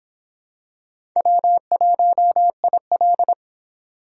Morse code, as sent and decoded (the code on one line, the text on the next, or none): .-- .---- .. .-..
W1IL